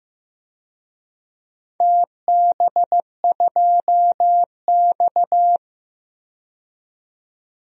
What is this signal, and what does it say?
Morse code, heard as a keyed tone, about 15 wpm, 700 Hz. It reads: TB2X
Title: DXCC-100 Call Signs - 4 Characters - 15wpm